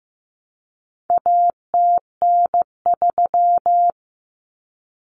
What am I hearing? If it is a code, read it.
Morse code, heard as a keyed tone, about 15 wpm, 700 Hz. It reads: ATN3
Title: Common Cut Numbers - 4-Digit 15wpm